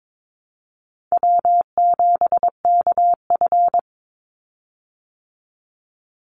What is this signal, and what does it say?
Morse code, heard as a keyed tone, about 22 wpm, 700 Hz. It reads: W7KF
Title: Call Signs - 22wpm